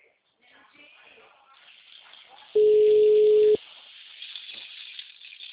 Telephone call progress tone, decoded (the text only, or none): ringback tone